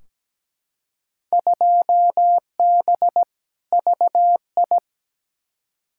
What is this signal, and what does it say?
Morse code, heard as a keyed tone, about 17 wpm, 700 Hz. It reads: 2B VI
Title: Field Day Exchanges Encore - 17wpm